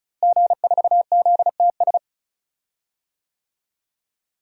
Morse code, heard as a keyed tone, about 36 wpm, 700 Hz. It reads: G4ZTS
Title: DXCC-100 Call Signs - 5 Characters - Repeated 3x - 35wpm